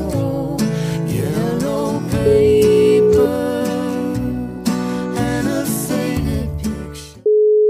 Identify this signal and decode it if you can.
Ringback tone (Europe, 425 Hz), with noise.